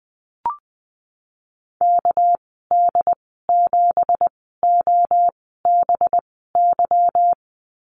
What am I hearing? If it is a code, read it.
Morse code, heard as a keyed tone, about 20 wpm, spaced as for 15 wpm, 700 Hz. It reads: KD7OBY
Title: 238 Lesson 38g - Letter q - Call Signs 20wpm